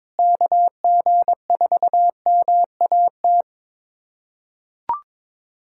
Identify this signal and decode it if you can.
Morse code, heard as a keyed tone, about 22 wpm, 700 Hz. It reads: KG4MAT
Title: USA Callsign DE Callsign 22wpm